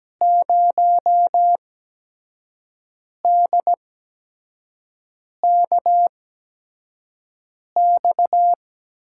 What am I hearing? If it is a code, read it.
Morse code, heard as a keyed tone, about 17 wpm, spaced as for 5 wpm, 700 Hz. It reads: 0DKX
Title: USA Call Signs - 6 Characters - 8x Character Spacing - 17wpm